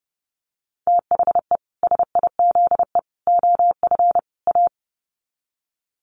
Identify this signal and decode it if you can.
Morse code, heard as a keyed tone, about 30 wpm, 700 Hz. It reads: THE SIZE OF A
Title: Sets of 4 Words 30wpm